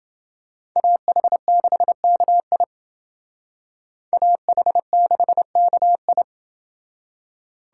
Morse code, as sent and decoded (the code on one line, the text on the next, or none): .- .... -.... -.- .. / .- .... -.... -.- ..
AH6KI AH6KI